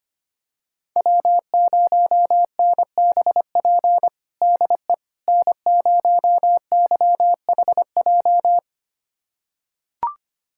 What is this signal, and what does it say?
Morse code, heard as a keyed tone, about 25 wpm, 700 Hz. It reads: W0NBP DE N0YHJ
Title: USA Callsign DE Callsign - 5 Characters - 25wpm